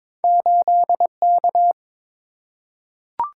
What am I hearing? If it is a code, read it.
Morse code, heard as a keyed tone, about 22 wpm, 700 Hz. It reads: 8K
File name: practice / 22wpm / Two Letter-Number Combinations - 22wpm